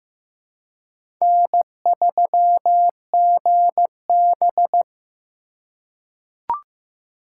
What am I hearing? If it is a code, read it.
Morse code, heard as a keyed tone, about 15 wpm, 700 Hz. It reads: N3GB
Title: USA Call Signs - 4 Characters - 3x Character Spacing - 15wpm